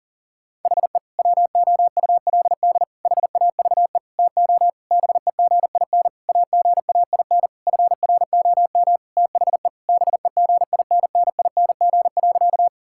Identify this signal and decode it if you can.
Morse code, heard as a keyed tone, about 40 wpm, 700 Hz. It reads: HE WOULD HAVE TO BEGIN AGAIN FROM THE BEGINNING.
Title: Sentences from Top 1000 Words - Review Part 5 - 40wpm